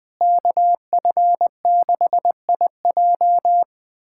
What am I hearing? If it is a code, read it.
Morse code, heard as a keyed tone, about 20 wpm, 700 Hz. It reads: KF6IJ
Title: USA Callsign DE Callsign - 5 Characters - 20wpm